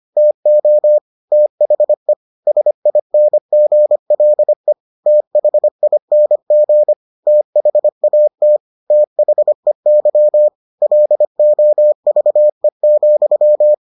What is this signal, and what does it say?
Morse code, heard as a keyed tone, about 25 wpm, 600 Hz. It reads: TO THE SINGLE THING THAT THEY LOVE,